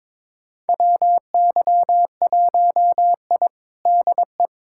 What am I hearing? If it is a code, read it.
Morse code, heard as a keyed tone, about 21 wpm, 700 Hz. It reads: WY1I DE
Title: USA Callsign DE Callsign - 4 Characters - 22wpm